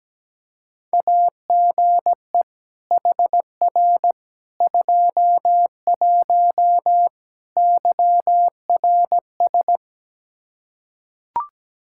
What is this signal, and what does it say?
Morse code, heard as a keyed tone, about 17 wpm, 700 Hz. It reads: AGE HR 21 YRS
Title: K1USN SST QSO Practice 17wpm